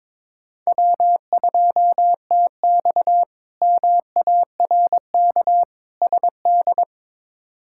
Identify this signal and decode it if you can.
Morse code, heard as a keyed tone, about 22 wpm, 700 Hz. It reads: W2TX MARK SD
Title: K1USN SST QSO Practice - 22wpm